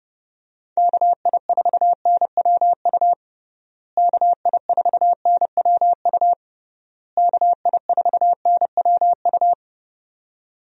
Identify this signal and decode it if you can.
Morse code, heard as a keyed tone, about 31 wpm, 700 Hz. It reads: KI4NWU KI4NWU KI4NWU
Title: USA Call Signs - 6 Characters - Repeated 3x - 30wpm